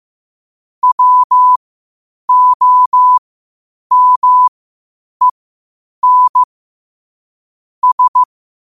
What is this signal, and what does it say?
Morse code, heard as a keyed tone, about 15 wpm, spaced as for 8 wpm, 1000 Hz. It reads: WOMEN S
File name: qotdfiles / 8wpm